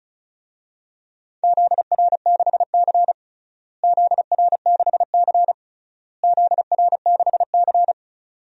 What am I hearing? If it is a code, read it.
Morse code, heard as a keyed tone, about 36 wpm, 700 Hz. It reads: ZR6C ZR6C ZR6C
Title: DXCC-100 Call Signs - 4 Characters - Repeated 3x - 35wpm